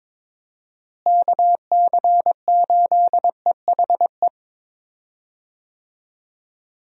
Morse code, heard as a keyed tone, about 22 wpm, 700 Hz. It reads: KC8EHE
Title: USA Call Signs - 6 Characters - Repeated 3x - 22wpm